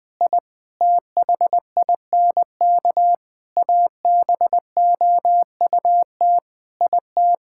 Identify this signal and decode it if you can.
Morse code, heard as a keyed tone, about 20 wpm, 700 Hz. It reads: I THINK ABOUT IT
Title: Sets of 9 Words - 20wpm